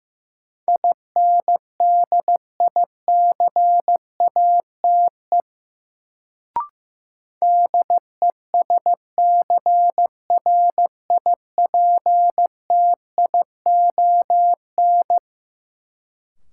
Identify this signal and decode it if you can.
Morse code, heard as a keyed tone, about 15 wpm, 700 Hz. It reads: INDICATE DESCRIPTION
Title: Sentences from Top 1300 Words - New Word Drill - 15wpm